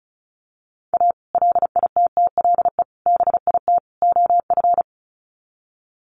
Morse code, heard as a keyed tone, about 34 wpm, 700 Hz. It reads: A LITTLE BIT OF
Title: Sets of 4 Words 35wpm